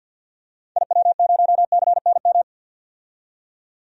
Morse code, heard as a keyed tone, about 45 wpm, 700 Hz. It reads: IW0CNM
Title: DXCC-100 Call Signs - 6 Characters - 50wpm